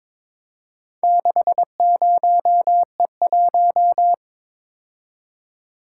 Morse code, heard as a keyed tone, about 22 wpm, 690 Hz. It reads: 60E1
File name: practice / 22wpm / Four Letter-Number Combinations - 22wpm